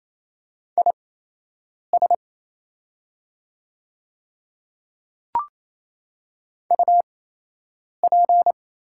Morse code, heard as a keyed tone, about 28 wpm, spaced as for 8 wpm, 700 Hz. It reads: IS UP
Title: 8x Character Spacing - Top 100 Words - 28wpm